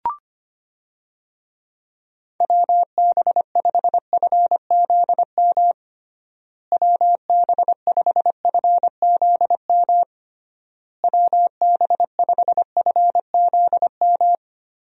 Morse code, heard as a keyed tone, about 25 wpm, 700 Hz. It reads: WB5FZM WB5FZM WB5FZM
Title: USA Call Signs - 6 Characters - Repeated 3x - 25wpm